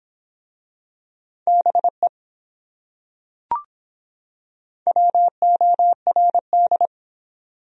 Morse code, heard as a keyed tone, about 22 wpm, 700 Hz. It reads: BE WORD